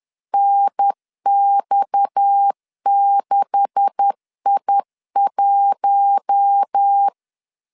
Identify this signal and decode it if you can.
Morse code, heard as a keyed tone, about 10 wpm, 800 Hz. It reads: NX6I1
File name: CW / DET / 1 9 1500C S A 10wpm 4511